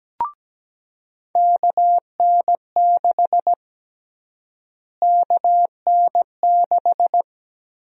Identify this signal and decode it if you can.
Morse code, heard as a keyed tone, about 17 wpm, 700 Hz. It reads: KN6 KN6
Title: USA Call Sign Prefixes - Repeated 2x - 17wpm